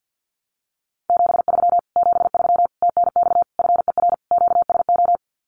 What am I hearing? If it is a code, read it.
Morse code, heard as a keyed tone, about 48 wpm, 700 Hz. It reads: 73 73 TNX FER QSO
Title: Common QSO Phrases Encore 50wpm